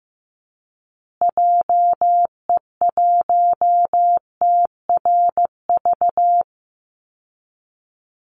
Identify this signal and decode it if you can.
Morse code, heard as a keyed tone, about 15 wpm, 700 Hz. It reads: JE1TRV
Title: Call Signs Repeated 2x - 15wpm